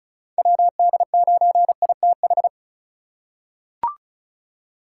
Morse code, heard as a keyed tone, about 34 wpm, 700 Hz. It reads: WD9ITH
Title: USA Call Signs - 6 Characters - 2x Character Spacing - 35wpm